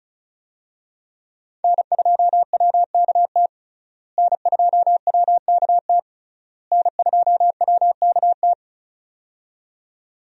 Morse code, heard as a keyed tone, about 36 wpm, 700 Hz. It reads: N2WKT N2WKT N2WKT
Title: USA Call Signs - 5 Characters - Repeated 3x - 35wpm